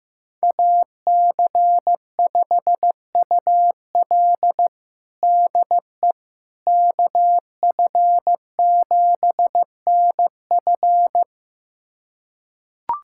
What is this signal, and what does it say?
Morse code, heard as a keyed tone, about 15 wpm, 700 Hz. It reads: AC5UL DE KF7NF
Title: USA Callsign DE Callsign 15wpm